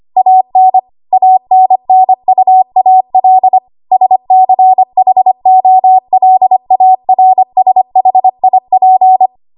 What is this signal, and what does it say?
Morse code, heard as a keyed tone, about 25 wpm, 750 Hz. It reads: AN ANNUAL SCHOLARSHIP
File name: Morse / 25WPM